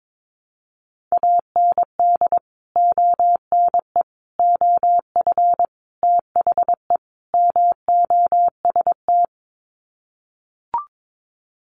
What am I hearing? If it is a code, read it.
Morse code, heard as a keyed tone, about 22 wpm, 700 Hz. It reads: AND ONE OF THE MOST
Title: Sets of 5 Words Encore 22wpm